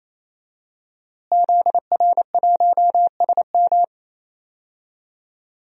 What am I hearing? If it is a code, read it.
Morse code, heard as a keyed tone, about 27 wpm, 700 Hz. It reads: ZR1SM
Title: DXCC-100 Call Signs - 5 Characters - Repeated 3x - 28wpm